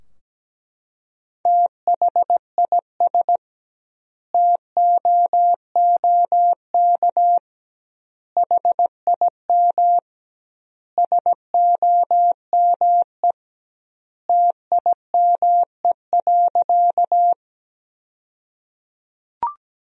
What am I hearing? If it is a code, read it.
Morse code, heard as a keyed tone, about 17 wpm, 700 Hz. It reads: THIS TOOK HIM SOME TIME.
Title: 2x Word-Spacing - Sentences from Top 300 Words - 17wpm